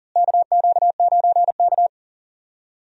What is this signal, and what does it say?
Morse code, heard as a keyed tone, about 40 wpm, 700 Hz. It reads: KQ9K
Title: USA Call Signs - 4 Characters - Repeated 2x - 40wpm